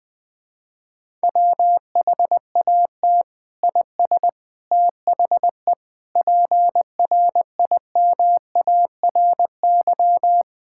Morse code, heard as a keyed tone, about 20 wpm, 700 Hz. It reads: WHAT IS THE PRIMARY
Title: Sentences from Top 1000 Words - Review Part 5 - 20wpm